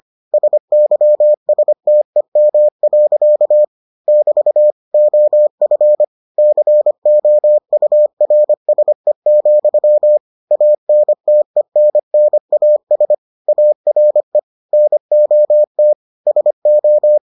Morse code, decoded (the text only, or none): SYSTEM. <BT> OF COURSE, ANTENNAS ARE NOT SO